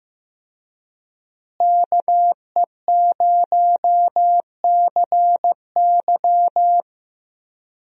Morse code, heard as a keyed tone, about 15 wpm, 700 Hz. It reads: KE0CY
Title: USA Call Signs - 5 Characters - Repeated 2x - 15wpm